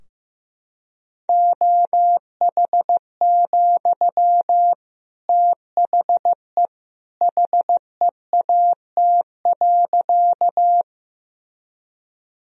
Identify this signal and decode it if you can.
Morse code, heard as a keyed tone, about 15 wpm, 700 Hz. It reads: OH, THE HEAT.
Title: Sentences from Top 1000 Words - Review Part 3 - 15wpm